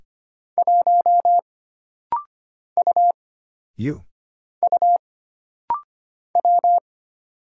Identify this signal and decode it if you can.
Morse code, heard as a keyed tone, about 25 wpm, 700 Hz. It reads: 1 U U W
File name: Single Character (Letter:Number:Punctuation:Prosign) - Rapid-Fire - 25wpm